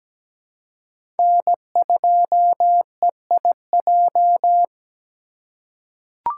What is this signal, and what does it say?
Morse code, heard as a keyed tone, about 17 wpm, 700 Hz. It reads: N2EIJ
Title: USA Call Signs - 5 Characters - 4x Character Spacing - 17wpm